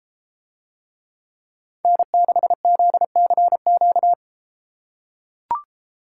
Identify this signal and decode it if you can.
Morse code, heard as a keyed tone, about 32 wpm, 700 Hz. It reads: N6ZCQ